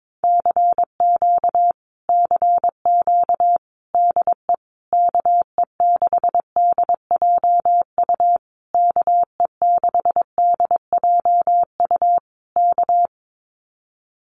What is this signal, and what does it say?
Morse code, heard as a keyed tone, about 22 wpm, 700 Hz. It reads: CQ CQ DE KE6DJU KE6DJU K
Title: Common QSO Phrases Encore 22wpm